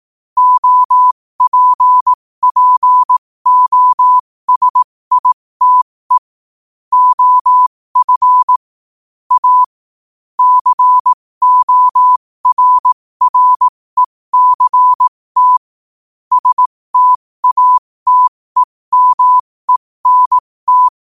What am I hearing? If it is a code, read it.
Morse code, heard as a keyed tone, about 18 wpm, spaced as for 15 wpm, 1000 Hz. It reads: OPPOSITE OF A CORRECT STATEMENT
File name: qotdfiles / 15wpm